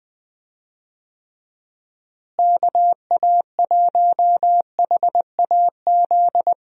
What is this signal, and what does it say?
Morse code, heard as a keyed tone, about 20 wpm, 700 Hz. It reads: KA1HAZ
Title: USA Call Signs - 6 Characters - Repeated 2x - 20wpm